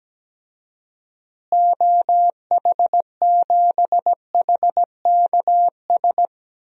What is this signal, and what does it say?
Morse code, heard as a keyed tone, about 17 wpm, 700 Hz. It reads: OH7HKS
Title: DXCC-100 Call Signs - 6 Characters - Repeated 3x - 17wpm